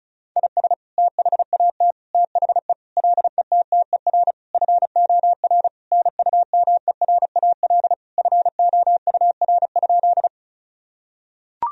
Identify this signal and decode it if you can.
Morse code, heard as a keyed tone, about 34 wpm, 700 Hz. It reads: IS THAT THE LETTER FOR NUMERAL FOUR?
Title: Sentences from Top 1000 Words - Review Part 2 - 35wpm